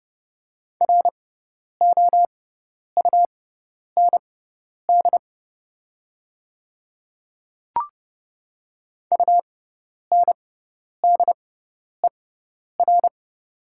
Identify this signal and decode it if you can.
Morse code, heard as a keyed tone, about 30 wpm, spaced as for 10 wpm, 700 Hz. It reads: ROUND UNDER
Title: 6x Character Spacing - Top 200 Words - 30wpm